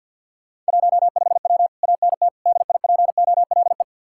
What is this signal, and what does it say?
Morse code, heard as a keyed tone, about 60 wpm, 700 Hz. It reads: 15W ANT DIPOLE
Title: Speed-Racing Common QSO Phrases - 50wpm